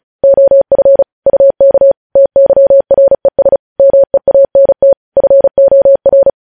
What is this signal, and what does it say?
Morse code, heard as a keyed tone, about 36 wpm, 550 Hz. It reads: OF UK TYRES MEANT FOR
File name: podcast / News headlines-1x-35x35